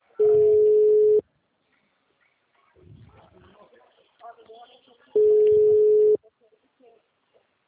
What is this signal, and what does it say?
Ringback tone (Europe, 425 Hz), with speech.
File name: out- -2017- - - .51719